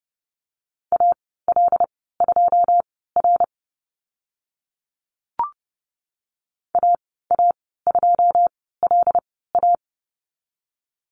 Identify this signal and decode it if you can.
Morse code, heard as a keyed tone, about 30 wpm, spaced as for 17 wpm, 700 Hz. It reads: AL2R AA2LA
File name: Lesson 14g - Number 2 - Call Signs 30wpm